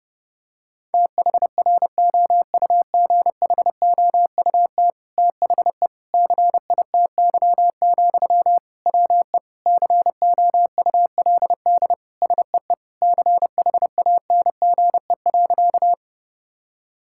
Morse code, heard as a keyed tone, about 30 wpm, 700 Hz. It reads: THROUGHOUT THE CITY, WE COULD SEE CHANGE.
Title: Sentences from Top 1000 Words - Review Part 5 - 30wpm